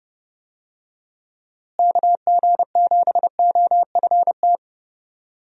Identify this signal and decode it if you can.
Morse code, heard as a keyed tone, about 30 wpm, 700 Hz. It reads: KG7OFT
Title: USA Call Signs - 6 Characters - Repeated 3x - 30wpm